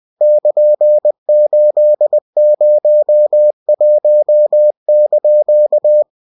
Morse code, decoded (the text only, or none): <KN>801)